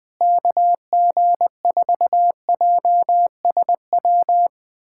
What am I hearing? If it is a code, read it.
Morse code, heard as a keyed tone, about 20 wpm, 700 Hz. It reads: KG4JSW
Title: SOTA QSO Practice 20wpm